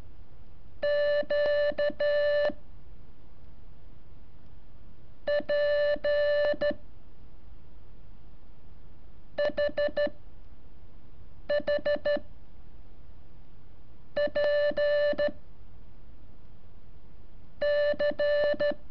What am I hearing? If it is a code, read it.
Morse code, heard as a keyed tone, about 13 wpm, 580 Hz. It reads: Q P H H P C
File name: curso / ejercicio 18